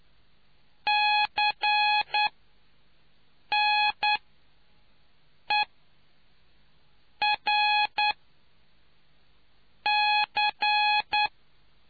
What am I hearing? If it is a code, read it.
Morse code, heard as a keyed tone, about 10 wpm, spaced as for 5 wpm, 840 Hz. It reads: CNERC